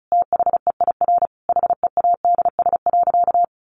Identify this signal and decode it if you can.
Morse code, heard as a keyed tone, about 34 wpm, 700 Hz. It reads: THEIR HEADS.
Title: A Sound Of Thunder 35wpm